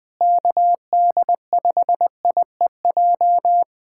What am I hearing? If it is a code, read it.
Morse code, heard as a keyed tone, about 20 wpm, 700 Hz. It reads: KD5IEJ
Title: USA Callsign DE Callsign - 6 Characters - 20wpm